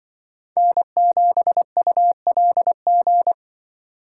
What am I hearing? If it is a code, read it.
Morse code, heard as a keyed tone, about 24 wpm, 700 Hz. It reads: N7ULG
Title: Speed Racing - USA Call Signs 20wpm